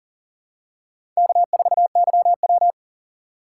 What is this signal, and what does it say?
Morse code, heard as a keyed tone, about 40 wpm, 700 Hz. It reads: K4YW
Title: USA Call Signs - 4 Characters - 3x Character Spacing - 40wpm